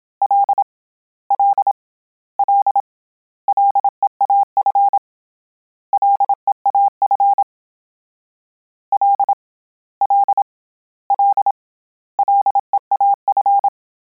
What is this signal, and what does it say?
Morse code, heard as a keyed tone, about 27 wpm, 800 Hz. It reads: L L L LEAF LEAF L L L LEAF
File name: Combinations / Combinations-mp3s / Combinations@25